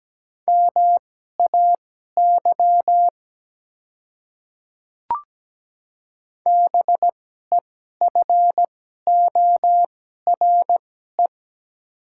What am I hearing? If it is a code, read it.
Morse code, heard as a keyed tone, about 17 wpm, spaced as for 12 wpm, 700 Hz. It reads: MAY BEFORE
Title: 2x Character Spacing - Top 300 Words - 17wpm